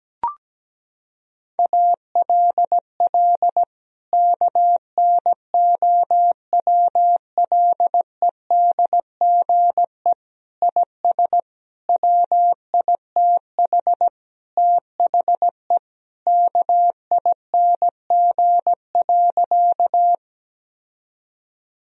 Morse code, heard as a keyed tone, about 17 wpm, 700 Hz. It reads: ALL KNOWLEDGE IS WITH THE KING.
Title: Sentences from Top 1000 Words - Review Part 3 - 17wpm